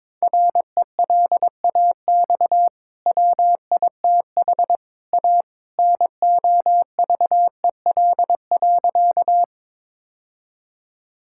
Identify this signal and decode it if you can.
Morse code, heard as a keyed tone, about 22 wpm, 700 Hz. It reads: RELAX WITH A NOVEL.
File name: practice / 22wpm / Easy Sentences with Hints - 22wpm